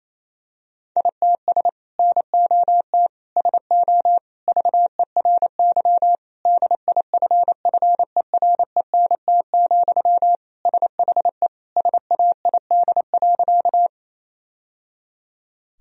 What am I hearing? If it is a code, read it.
Morse code, heard as a keyed tone, about 28 wpm, 700 Hz. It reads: ITS NOT SO VERY DIFFERENT, SHE SAID.
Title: Sentences from Top 1000 Words - Review Part 3 - 28wpm